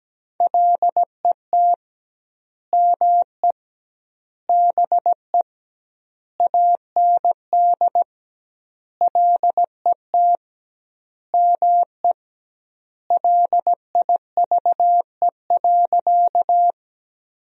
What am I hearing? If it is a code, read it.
Morse code, heard as a keyed tone, about 17 wpm, 700 Hz. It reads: LET ME BE AND LET ME LIVE.
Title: 2x Word-Spacing - Sentences from Top 300 Words - 17wpm